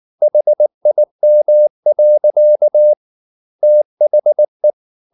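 Morse code, heard as a keyed tone, about 19 wpm, 600 Hz. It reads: HIM. THE
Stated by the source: Morse code at 19 wpm from CW med Gnister, reading Ken Follett's novel Never in English.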